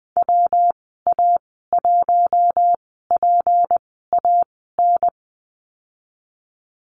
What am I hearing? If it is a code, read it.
Morse code, heard as a keyed tone, about 20 wpm, spaced as for 14 wpm, 700 Hz. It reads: WA1PAN
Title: 167 Lesson 27g - Number 7 - Call Signs 20wpm